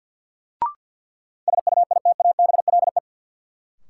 Morse code, heard as a keyed tone, about 53 wpm, 700 Hz. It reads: SUITABLE
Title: Sentences from Top 1400 Words - New Word Drill Alt - 50wpm